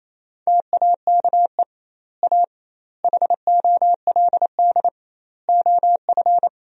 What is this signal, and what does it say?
Morse code, heard as a keyed tone, about 28 wpm, 700 Hz. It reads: TAKE A HOLD OF
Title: 2x Word-Spacing - Sentences from Top 400 Words - 28wpm